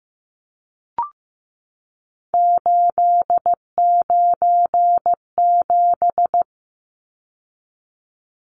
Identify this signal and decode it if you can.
Morse code, heard as a keyed tone, about 15 wpm, 700 Hz. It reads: 897
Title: Numbers - 3-Digit 15wpm